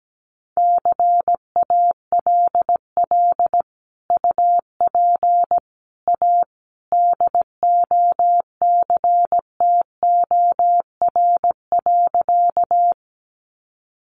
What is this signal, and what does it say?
Morse code, heard as a keyed tone, about 17 wpm, 700 Hz. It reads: CALL UP A DOCTOR.
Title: Sentences from Top 1500 Words 17wpm